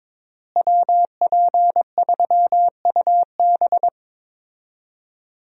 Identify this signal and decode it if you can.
Morse code, heard as a keyed tone, about 22 wpm, 700 Hz. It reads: WP3UB
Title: USA Call Signs - 5 Characters - Repeated 2x - 22wpm